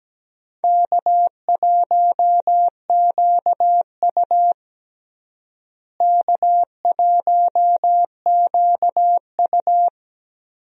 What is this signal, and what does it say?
Morse code, heard as a keyed tone, about 17 wpm, 700 Hz. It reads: K1QU K1QU
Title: USA Call Signs - 4 Characters - Repeated 3x - 17wpm